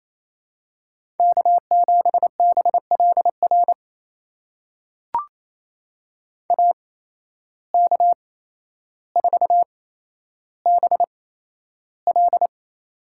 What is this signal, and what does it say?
Morse code, heard as a keyed tone, about 28 wpm, spaced as for 12 wpm, 700 Hz. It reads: K7BLR AK4BL
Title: USA Call Signs - 5 Characters - 8x Character Spacing - 28wpm